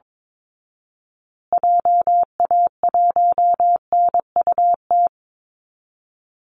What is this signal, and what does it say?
Morse code, heard as a keyed tone, about 22 wpm, 700 Hz. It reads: JA1NUT
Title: Call Signs Repeated 2x - 22wpm